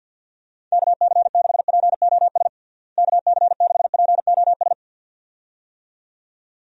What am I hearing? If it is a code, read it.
Morse code, heard as a keyed tone, about 49 wpm, 700 Hz. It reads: KC6POS KC6POS
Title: USA Call Signs - 6 Characters - Repeated 2x - 50wpm